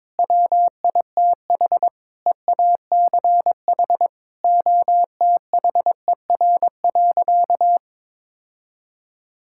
Morse code, heard as a keyed tone, about 22 wpm, 700 Hz. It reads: WITH EACH OTHER.
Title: Sentences from Top 1000 Words - Review Part 6 - 22wpm